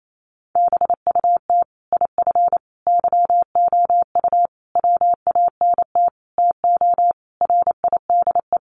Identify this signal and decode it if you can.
Morse code, heard as a keyed tone, about 28 wpm, 700 Hz. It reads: BUT IF YOU WANT TO RIDE
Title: Sentences from Top 1200 Words 28wpm